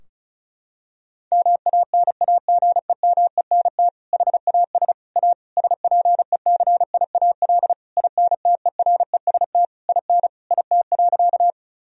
Morse code, heard as a keyed tone, about 34 wpm, 700 Hz. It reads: MANAGEMENT HAS A SPECIAL INTEREST IN IT.
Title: Sentences from Top 1000 Words - Review Part 4 - 35wpm